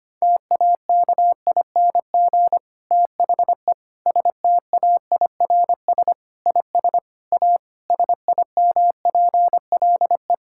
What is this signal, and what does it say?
Morse code, heard as a keyed tone, about 25 wpm, 700 Hz. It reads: TAKING THE STAIRS IS A SIMPLE
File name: practice / 25wpm / Difficult Sentences with Hints - 25wpm